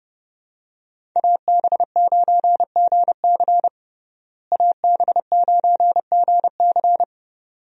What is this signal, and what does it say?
Morse code, heard as a keyed tone, about 32 wpm, 700 Hz. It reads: AB9GC AB9GC
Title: USA Call Signs - 5 Characters - Repeated 2x - 30wpm